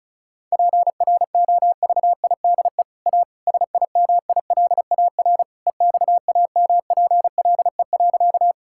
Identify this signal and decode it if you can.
Morse code, heard as a keyed tone, about 36 wpm, 700 Hz. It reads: PROVIDE A SIMILAR EXAMPLE.
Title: Sentences from Top 1000 Words - Review Part 5 - 35wpm